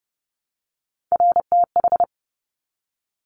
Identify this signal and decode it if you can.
Morse code, heard as a keyed tone, about 30 wpm, 700 Hz. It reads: RTH
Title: Top Three Letter Combinations in Words 30wpm